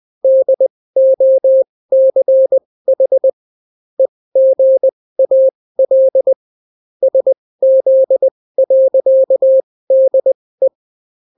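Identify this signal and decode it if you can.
Morse code, heard as a keyed tone, about 20 wpm, spaced as for 16 wpm, 530 Hz. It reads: DOCH EGAL SZ.DE
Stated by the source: Daily Morse weather and top stories at 16 wpm, clean without QRN.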